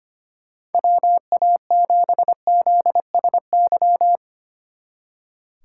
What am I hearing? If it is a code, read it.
Morse code, heard as a keyed tone, about 30 wpm, 700 Hz. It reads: WA7ZSY
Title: Speed Racing - USA 2x3 Call Signs 25wpm